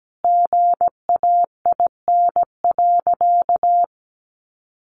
Morse code, heard as a keyed tone, about 17 wpm, 700 Hz. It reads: GAIN.
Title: Sentences from Top 1100 Words 17wpm